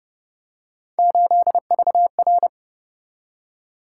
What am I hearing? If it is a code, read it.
Morse code, heard as a keyed tone, about 29 wpm, 700 Hz. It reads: 8VR